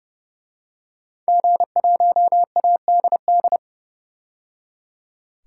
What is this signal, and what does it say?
Morse code, heard as a keyed tone, about 29 wpm, 700 Hz. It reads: G1ADD
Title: DXCC-100 Call Signs - 5 Characters - 30wpm